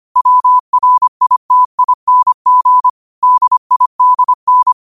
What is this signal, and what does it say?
Morse code, heard as a keyed tone, about 25 wpm, 1000 Hz. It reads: WRITING DIDN
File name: qotdfiles / 25wpm